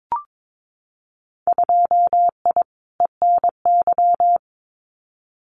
Morse code, heard as a keyed tone, about 22 wpm, 700 Hz. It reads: 2I ENY
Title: Winter Field Day Exchanges Encore 22wpm